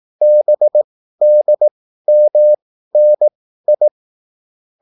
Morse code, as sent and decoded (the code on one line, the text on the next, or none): -... -.. -- -. ..
BDMNI